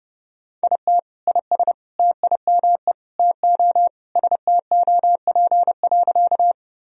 Morse code, heard as a keyed tone, about 32 wpm, 700 Hz. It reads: IT IS TIME TO STOP.
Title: Sentences from Top 1000 Words - Review Part 1 - 30wpm